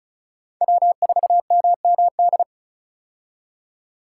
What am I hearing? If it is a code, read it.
Morse code, heard as a keyed tone, about 36 wpm, 700 Hz. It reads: W4MMD
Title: K1USN SST QSO Practice - 35wpm